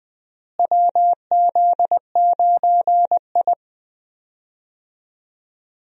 Morse code, heard as a keyed tone, about 21 wpm, 700 Hz. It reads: WZ9I
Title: USA Call Signs - 4 Characters - Repeated 3x - 20wpm